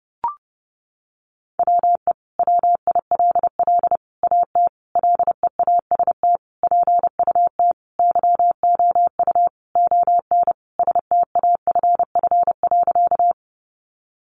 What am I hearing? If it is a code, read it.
Morse code, heard as a keyed tone, about 30 wpm, 700 Hz. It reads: WE WILL AT LEAST PUT YOU ON STAFF.
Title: Sentences from Top 1200 Words 30wpm